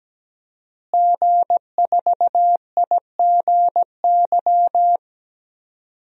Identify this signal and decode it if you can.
Morse code, heard as a keyed tone, about 17 wpm, 700 Hz. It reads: G4IGY
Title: DXCC-100 Call Signs - 5 Characters - Repeated 2x - 17wpm